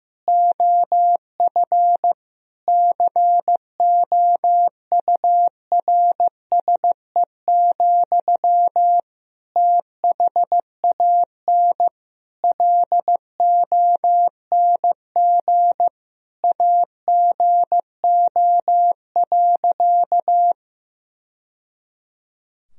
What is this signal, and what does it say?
Morse code, heard as a keyed tone, about 16 wpm, 700 Hz. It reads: OF COURSE, THAN LONG AGO.
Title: Sentences from Top 1000 Words - Review Part 3 - 15wpm